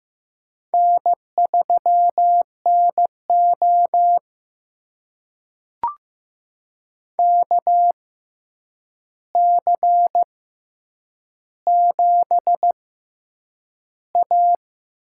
Morse code, heard as a keyed tone, about 15 wpm, spaced as for 8 wpm, 700 Hz. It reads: N3NO KC7A
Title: USA Call Signs - 4 Characters - 6x Character Spacing - 15wpm